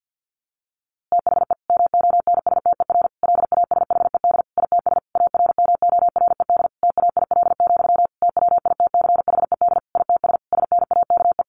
Morse code, heard as a keyed tone, about 50 wpm, 700 Hz. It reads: THE MONSTER LASHED ITS ARMORED TAIL, TWITCHED ITS SNAKE
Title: A Sound Of Thunder 50wpm